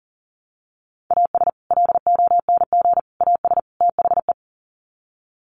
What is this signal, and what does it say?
Morse code, heard as a keyed tone, about 40 wpm, 700 Hz. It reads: AS LONG AS THE
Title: Sets of 4 Words 40wpm